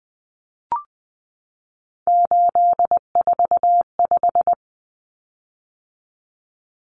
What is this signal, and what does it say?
Morse code, heard as a keyed tone, about 20 wpm, 700 Hz. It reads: 845
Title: Common Cut Numbers - 3-Digit 20wpm